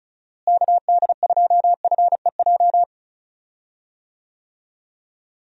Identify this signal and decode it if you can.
Morse code, heard as a keyed tone, about 36 wpm, 700 Hz. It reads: KD2FEJ
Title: USA Call Signs - 6 Characters - Repeated 2x - 35wpm